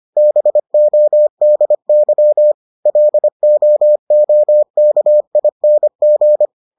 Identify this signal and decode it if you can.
Morse code, heard as a keyed tone, about 25 wpm, 600 Hz. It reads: BODY LOOKING